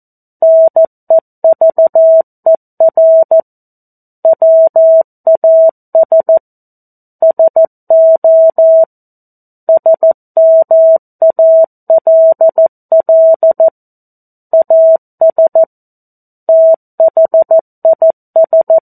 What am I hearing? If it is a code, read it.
Morse code, heard as a keyed tone, about 14 wpm, 650 Hz. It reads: NEVER WAS SO SMALL AS THIS
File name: 02-Chapter-02 14 WPM